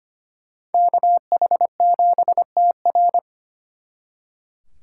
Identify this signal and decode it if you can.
Morse code, heard as a keyed tone, about 33 wpm, 700 Hz. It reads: KH7TR